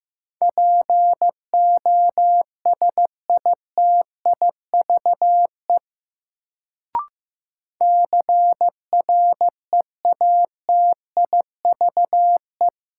Morse code, heard as a keyed tone, about 15 wpm, 700 Hz. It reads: POSITIVE CREATIVE
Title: Sentences from Top 1000 Words - Review Part 5 - New Word Drill - 15wpm